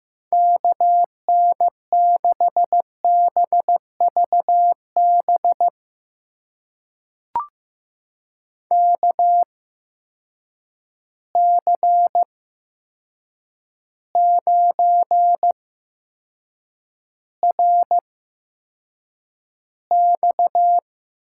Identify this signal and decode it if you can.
Morse code, heard as a keyed tone, about 15 wpm, spaced as for 7 wpm, 700 Hz. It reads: KN6BVB KC9RX
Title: USA Call Signs - 6 Characters - 8x Character Spacing - 15wpm